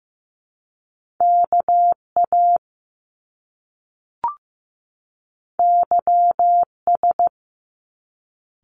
Morse code, heard as a keyed tone, about 15 wpm, 700 Hz. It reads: KA YS
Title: Top Two Letter Combinations in Words 15wpm